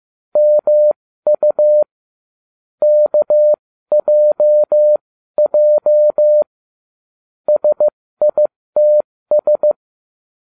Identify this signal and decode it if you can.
Morse code, heard as a keyed tone, about 15 wpm, spaced as for 13 wpm, 600 Hz. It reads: MU KJJ SITS